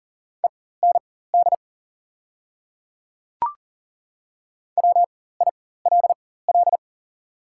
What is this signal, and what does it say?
Morse code, heard as a keyed tone, about 40 wpm, spaced as for 19 wpm, 690 Hz. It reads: END WILL
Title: 4x Character Spacing - Top 200 Words - 40wpm